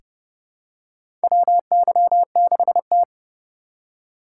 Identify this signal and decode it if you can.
Morse code, heard as a keyed tone, about 30 wpm, 700 Hz. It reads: WY6T
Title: USA Call Signs - 4 Characters - 4x Character Spacing - 30wpm